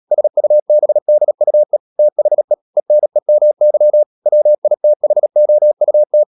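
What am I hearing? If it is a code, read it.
Morse code, heard as a keyed tone, about 37 wpm, 600 Hz. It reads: SUBDUE THE ENEMY WITHOUT
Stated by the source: Morse code practice at 35–39 wpm sending English text of Ken Follett's novel Never, from CW med Gnister.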